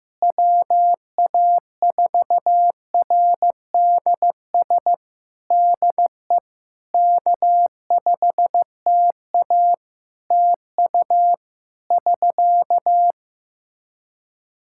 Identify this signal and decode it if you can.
Morse code, heard as a keyed tone, about 15 wpm, 700 Hz. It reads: WA4RDS DE K5TA TU <SK>
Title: Practice QSO - Exchange 4 - Ending - 15wpm